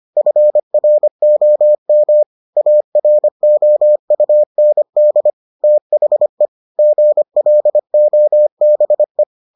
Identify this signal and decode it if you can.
Morse code, heard as a keyed tone, about 25 wpm, 600 Hz. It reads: FROM AROUND THE GLOBE